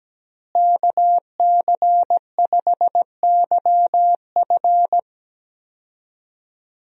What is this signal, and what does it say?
Morse code, heard as a keyed tone, about 17 wpm, 700 Hz. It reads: KC5YF
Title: USA Call Signs - 5 Characters - Repeated 2x - 17wpm